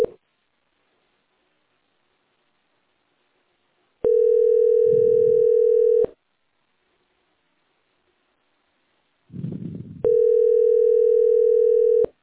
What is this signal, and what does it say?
Ringback tone (North America), 440 + 480 Hz.
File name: 9c896d41-5495-4931-bf80